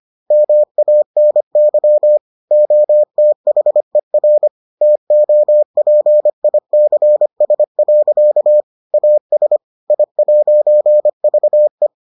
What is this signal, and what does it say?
Morse code, heard as a keyed tone, about 25 wpm, 600 Hz. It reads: MANY OTHER TOPICS. AS I'VE